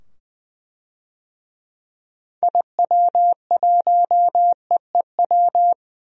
Morse code, heard as a keyed tone, about 20 wpm, 700 Hz. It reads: IW1EEW
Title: DXCC-100 Call Signs - 6 Characters - 20wpm